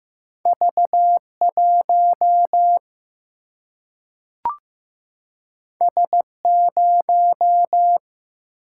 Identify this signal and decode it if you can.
Morse code, heard as a keyed tone, about 15 wpm, 700 Hz. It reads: V1 S0
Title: Single-Letter Single-Number 15wpm